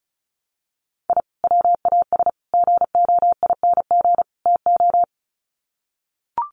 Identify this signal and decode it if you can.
Morse code, heard as a keyed tone, about 34 wpm, 700 Hz. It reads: I WAS GOING TO
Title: Sets of 4 Words 35wpm